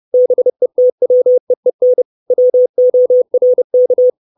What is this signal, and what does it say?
Morse code, heard as a keyed tone, about 30 wpm, 500 Hz. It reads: BETWEEN WORK
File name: 632 Words 30WPM